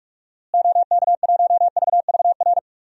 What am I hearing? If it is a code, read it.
Morse code, heard as a keyed tone, about 46 wpm, 700 Hz. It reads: OK1VVR
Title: DXCC-100 Call Signs - 6 Characters - Repeated 3x - 45wpm